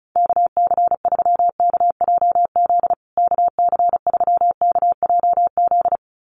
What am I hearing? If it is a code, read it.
Morse code, heard as a keyed tone, about 34 wpm, 700 Hz. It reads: KC3KJZ KC3KJZ
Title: USA Call Signs Repeated 2x 35wpm